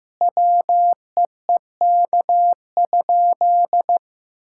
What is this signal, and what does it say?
Morse code, heard as a keyed tone, about 15 wpm, 700 Hz. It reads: WEEK?
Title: Sentences from Top 1000 Words - Review Part 6 - 15wpm